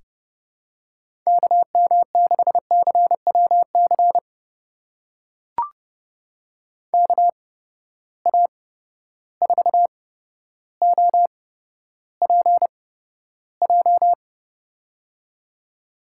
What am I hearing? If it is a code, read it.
Morse code, heard as a keyed tone, about 31 wpm, spaced as for 13 wpm, 700 Hz. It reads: KM6CWC KA4OPJ
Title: USA Call Signs - 6 Characters - 8x Character Spacing - 30wpm